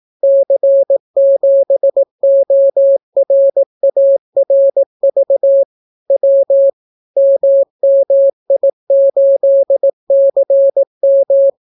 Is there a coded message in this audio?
Yes: Morse code from 0.2 to 11.5 s, heard as a keyed tone, about 18 wpm, 550 Hz. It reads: C7ORARV W MMI8CM